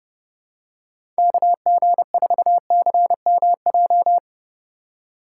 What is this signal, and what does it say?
Morse code, heard as a keyed tone, about 31 wpm, 700 Hz. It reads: KG4CMJ
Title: USA Call Signs - 2x Character Spacing - 30wpm